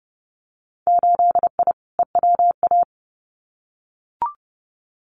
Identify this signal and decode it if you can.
Morse code, heard as a keyed tone, about 30 wpm, 700 Hz. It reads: 8I EWA
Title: Winter Field Day Exchanges Encore 30wpm